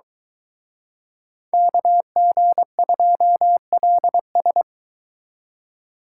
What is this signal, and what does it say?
Morse code, heard as a keyed tone, about 24 wpm, 700 Hz. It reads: KG2LS